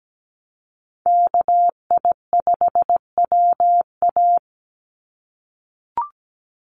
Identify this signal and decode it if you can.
Morse code, heard as a keyed tone, about 17 wpm, 700 Hz. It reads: KI5WA
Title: NA SOTA Chaser Callsigns 17wpm